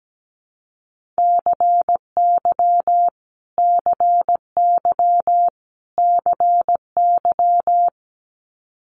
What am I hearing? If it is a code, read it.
Morse code, heard as a keyed tone, about 17 wpm, 700 Hz. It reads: CY CY CY
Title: Multi-letter Phonemes Repeated 3x 17wpm